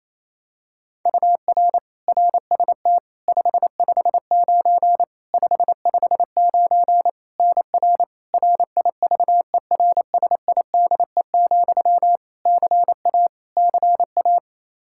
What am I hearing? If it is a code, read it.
Morse code, heard as a keyed tone, about 28 wpm, 700 Hz. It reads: UR RST 559 559 NR RIVERSIDE, CA CA
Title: Practice QSO - Exchange 1 - Beginning - 28wpm